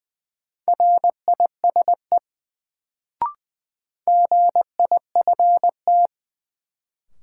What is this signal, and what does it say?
Morse code, heard as a keyed tone, about 20 wpm, 700 Hz. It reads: RISE GIFT
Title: Sentences from Top 900 Words - New Word Drill - 20wpm